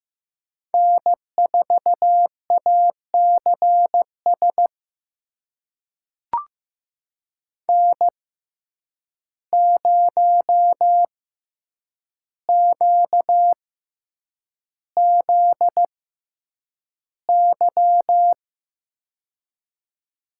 Morse code, heard as a keyed tone, about 15 wpm, spaced as for 8 wpm, 700 Hz. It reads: N4ACS N0QZY
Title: USA Call Signs - 5 Characters - 6x Character Spacing - 15wpm